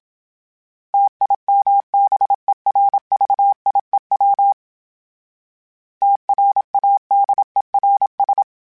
Morse code, heard as a keyed tone, about 27 wpm, 800 Hz. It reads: TIMBERVIEW TRADERS
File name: fromSMs / FromSMs mp3s / Cities@25